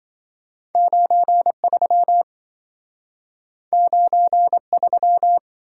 Morse code, heard as a keyed tone, about 27 wpm, 700 Hz. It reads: 93 93